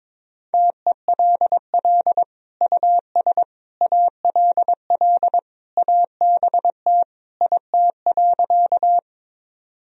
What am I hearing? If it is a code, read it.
Morse code, heard as a keyed tone, about 22 wpm, 700 Hz. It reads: TELL US ALL ABT IT.
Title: Sentences from Top 200 Words and Abbreviations 22wpm